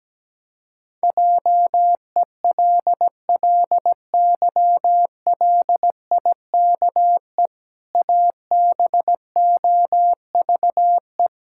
Morse code, heard as a keyed tone, about 17 wpm, 700 Hz. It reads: JELLYLIKE ABOVE
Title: Pangram Challenge v2 - 17wpm